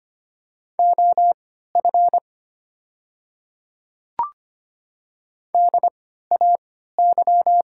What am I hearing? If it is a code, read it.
Morse code, heard as a keyed tone, about 25 wpm, spaced as for 14 wpm, 700 Hz. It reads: OF DAY
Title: 3x Character Spacing - Top 100 Words - 25wpm